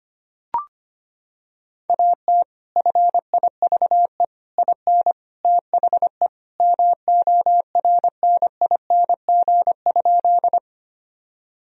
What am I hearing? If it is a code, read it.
Morse code, heard as a keyed tone, about 25 wpm, 700 Hz. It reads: AT FIVE IN THE MORNING?
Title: Sentences from Top 1000 Words - Review Part 2 - 25wpm